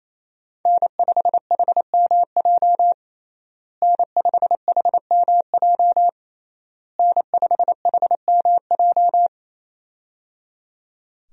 Morse code, heard as a keyed tone, about 28 wpm, 700 Hz. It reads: N5HMJ N5HMJ N5HMJ
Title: USA Call Signs - 5 Characters - Repeated 3x - 28wpm